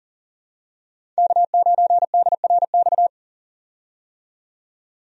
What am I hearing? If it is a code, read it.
Morse code, heard as a keyed tone, about 40 wpm, 700 Hz. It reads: K9DRX
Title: USA Call Signs - 5 Characters - 40wpm